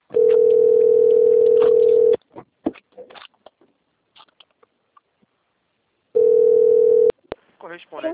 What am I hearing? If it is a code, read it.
Ringback tone (North America), 440 + 480 Hz, with speech.
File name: rg-1600-unknown- - - .15042